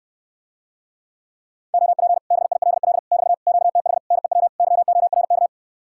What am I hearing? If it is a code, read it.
Morse code, heard as a keyed tone, about 67 wpm, 700 Hz. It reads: OP BILL <BT> QTH NEW YORK